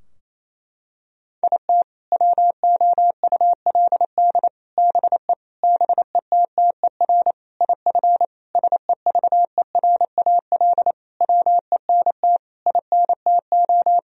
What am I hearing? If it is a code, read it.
Morse code, heard as a keyed tone, about 27 wpm, 700 Hz. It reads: IT WOULD BE BETTER IF SEVERAL WENT INTO
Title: Sentences from Top 1000 Words - Review Part 2 - 28wpm